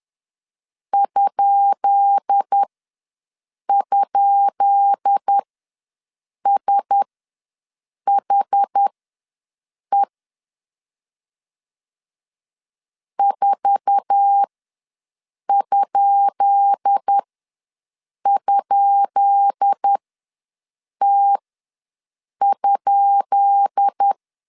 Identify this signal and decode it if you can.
Morse code, heard as a keyed tone, about 10 wpm, spaced as for 6 wpm, 800 Hz. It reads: ??SHE 4??T?
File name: LEZIONI / 10 SIGNS / 10B INT ws 10wpm 4134